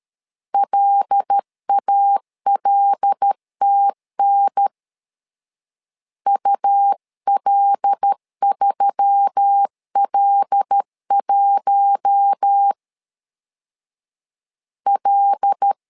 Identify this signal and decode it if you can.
Morse code, heard as a keyed tone, about 13 wpm, 800 Hz. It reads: LALTN UL3L1 L
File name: LEZIONI / 5 LBVC / 5 L 12wpm 1913